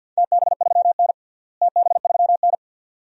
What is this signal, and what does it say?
Morse code, heard as a keyed tone, about 53 wpm, 700 Hz. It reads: TB3N TB3N
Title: DXCC-100 Call Signs - 4 Characters - Repeated 2x - 50wpm